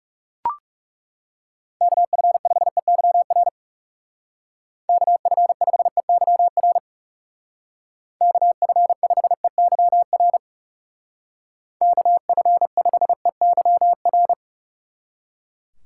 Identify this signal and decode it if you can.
Morse code, heard as a keyed tone, about 40 wpm, 700 Hz. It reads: KF5EYR KF5EYR KF5EYR KF5EYR